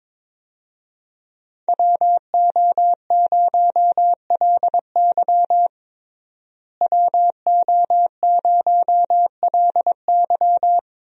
Morse code, heard as a keyed tone, about 22 wpm, 700 Hz. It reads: WO0LY WO0LY
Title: USA Call Signs - 5 Characters - Repeated 3x - 22wpm